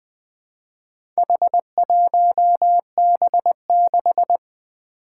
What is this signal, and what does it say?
Morse code, heard as a keyed tone, about 20 wpm, 700 Hz. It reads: H1B6
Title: Four Challenging Letter-Number Combinations 20wpm